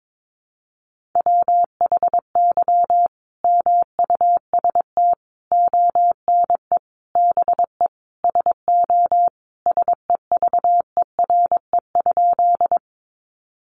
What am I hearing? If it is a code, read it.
Morse code, heard as a keyed tone, about 22 wpm, 700 Hz. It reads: WHY MUST ONE BE SO SEVERE?
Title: Sentences from Top 1400 Words 22wpm